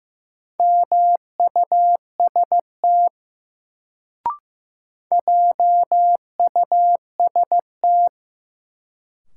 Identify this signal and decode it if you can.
Morse code, heard as a keyed tone, about 15 wpm, 700 Hz. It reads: MUST JUST
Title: Sentences from Top 200 Words - New Word Drill - 15wpm